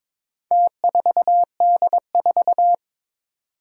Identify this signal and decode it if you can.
Morse code, heard as a keyed tone, about 22 wpm, 700 Hz. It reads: T4D4
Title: Four Challenging Letter-Number Combinations 22wpm